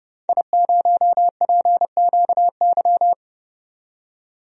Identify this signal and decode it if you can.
Morse code, heard as a keyed tone, about 30 wpm, 700 Hz. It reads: I0PQY
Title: DXCC-100 Call Signs - 5 Characters - 30wpm